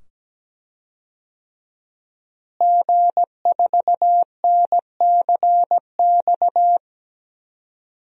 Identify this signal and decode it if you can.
Morse code, heard as a keyed tone, about 17 wpm, 700 Hz. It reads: G4NCX
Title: DXCC-100 Call Signs - 5 Characters - Repeated 2x - 17wpm